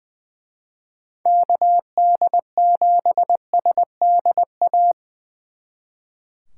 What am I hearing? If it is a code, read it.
Morse code, heard as a keyed tone, about 23 wpm, 700 Hz. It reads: KD7SDA
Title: Speed Racing - USA 2x3 Call Signs 20wpm